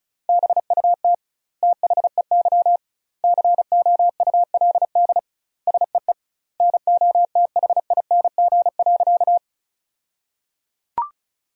Morse code, heard as a keyed tone, about 36 wpm, 700 Hz. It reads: BUT THEY COULD SEE NOTHING.
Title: 2x Word-Spacing - Sentences from Top 500 Words - 35wpm